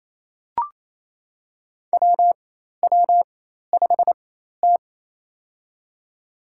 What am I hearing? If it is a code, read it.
Morse code, heard as a keyed tone, about 27 wpm, spaced as for 13 wpm, 700 Hz. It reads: WW5T
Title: USA Call Signs - 4 Characters - 4x Character Spacing - 28wpm